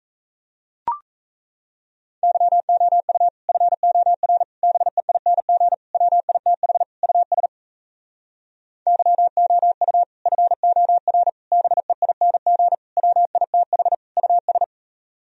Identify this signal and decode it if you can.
Morse code, heard as a keyed tone, about 38 wpm, 700 Hz. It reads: YOU FOR BEING WITH US YOU FOR BEING WITH US